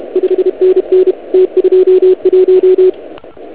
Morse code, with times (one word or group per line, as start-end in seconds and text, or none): 0.2-1.1 5NN
1.3-2.9 T21